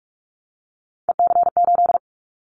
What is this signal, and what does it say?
Morse code, heard as a keyed tone, about 46 wpm, 700 Hz. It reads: EC8
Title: Callsign Prefixes for Top 50 DXCC - 45wpm